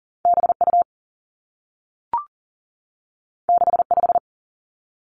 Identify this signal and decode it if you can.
Morse code, heard as a keyed tone, about 42 wpm, 700 Hz. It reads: B U 6 5
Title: Two Challenging Letter-Number Combinations 40wpm